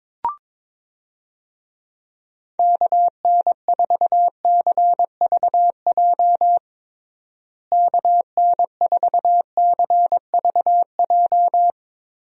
Morse code, heard as a keyed tone, about 21 wpm, 700 Hz. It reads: KN4CVJ KN4CVJ
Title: USA Call Signs - 6 Characters - Repeated 2x - 22wpm